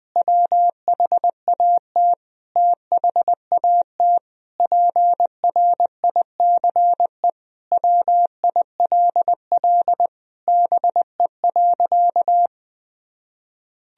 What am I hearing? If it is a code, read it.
Morse code, heard as a keyed tone, about 21 wpm, 700 Hz. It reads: WHAT THAT PRICE WILL BE.
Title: Sentences from Top 1000 Words - Review Part 4 - 20wpm